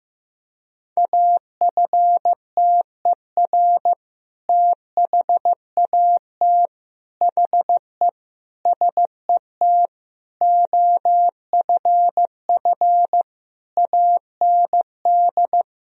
Morse code, heard as a keyed tone, about 15 wpm, 700 Hz. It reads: AFTER THAT HE SET OFF AND
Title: Sentences from Top 1000 Words - Review Part 1 - 15wpm